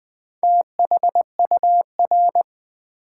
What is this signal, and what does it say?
Morse code, heard as a keyed tone, about 20 wpm, 700 Hz. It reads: THUR
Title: Build A Word - 2 Letters At A Time - 8-Letter Words - 20wpm